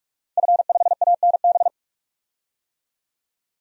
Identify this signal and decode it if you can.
Morse code, heard as a keyed tone, about 46 wpm, 700 Hz. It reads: F5ANB
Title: DXCC-100 Call Signs - 5 Characters - 45wpm